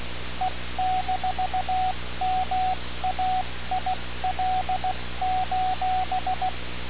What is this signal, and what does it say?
Morse code, heard as a keyed tone, about 16 wpm, 740 Hz, with radio noise. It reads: E-MAIL:
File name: radiocap / REC z